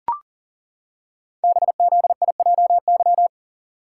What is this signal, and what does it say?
Morse code, heard as a keyed tone, about 39 wpm, 700 Hz. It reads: BZIJY
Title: Code Groups - 5 Letters - 40wpm